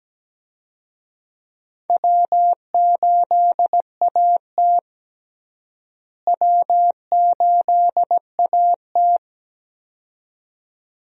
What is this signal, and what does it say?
Morse code, heard as a keyed tone, about 17 wpm, 700 Hz. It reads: W8AT W8AT
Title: USA Call Signs - 4 Characters - Repeated 2x - 17wpm